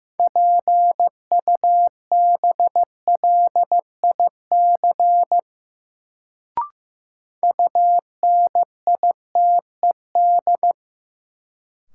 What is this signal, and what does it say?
Morse code, heard as a keyed tone, about 15 wpm, 700 Hz. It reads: PUBLIC UNITED
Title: Sentences from Top 600 Words - New Word Drill - 15wpm